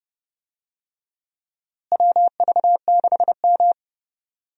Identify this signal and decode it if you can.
Morse code, heard as a keyed tone, about 30 wpm, 700 Hz. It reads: WV6M
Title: USA Call Signs - 4 Characters - Repeated 3x - 30wpm